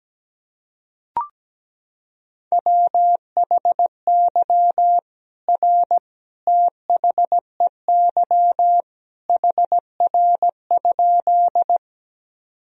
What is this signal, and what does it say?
Morse code, heard as a keyed tone, about 17 wpm, 700 Hz. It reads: WHY R THEY HR?
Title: Sentences from Top 200 Words and Abbreviations 17wpm